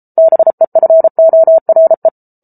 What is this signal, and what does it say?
Morse code, heard as a keyed tone, about 34 wpm, 650 Hz. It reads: BEFORE